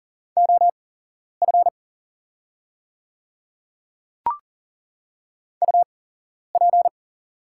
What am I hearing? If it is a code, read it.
Morse code, heard as a keyed tone, about 40 wpm, spaced as for 11 wpm, 700 Hz. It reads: OF UP
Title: 8x Character Spacing - Top 100 Words - 40wpm